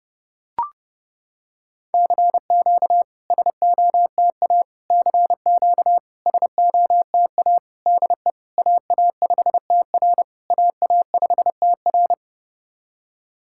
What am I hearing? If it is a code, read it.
Morse code, heard as a keyed tone, about 28 wpm, 690 Hz. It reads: CQ SOTA CQ SOTA DE AA5TR AA5TR
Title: SOTA QSO Practice 30wpm